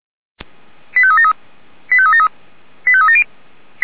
Five-tone selective call (ZVEI), heard as quick runs of five tones, tones of about 70 ms. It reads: 85373 85373 85370
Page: audio.f85360.feuerwehr-orlamuende.org